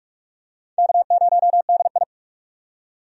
Morse code, heard as a keyed tone, about 44 wpm, 700 Hz. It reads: K0DI
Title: USA Call Signs - 4 Characters - 45wpm